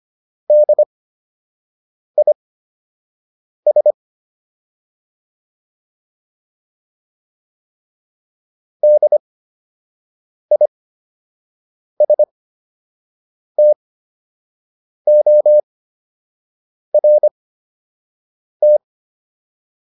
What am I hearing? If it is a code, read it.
Morse code, heard as a keyed tone, about 25 wpm, spaced as for 6 wpm, 600 Hz. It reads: DIS DISTORT